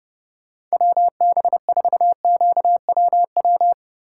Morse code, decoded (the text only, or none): WB4QWW